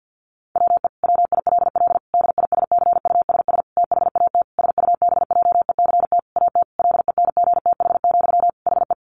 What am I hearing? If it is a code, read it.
Morse code, heard as a keyed tone, about 50 wpm, 700 Hz. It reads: WE WILL DISCUSS THAT SUBJECT AT LENGTH, HE
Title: Sentences from Top 1000 Words 50wpm